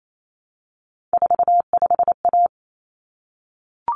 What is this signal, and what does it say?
Morse code, heard as a keyed tone, about 28 wpm, 700 Hz. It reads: 45A
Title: Common Cut Numbers - 3-Digit 28wpm